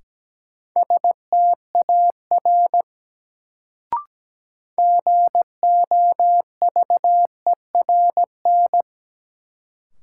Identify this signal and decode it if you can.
Morse code, heard as a keyed tone, about 17 wpm, 700 Hz. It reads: STAR GOVERN
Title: Sentences from Top 1000 Words - Review Part 2 - New Word Drill - 17wpm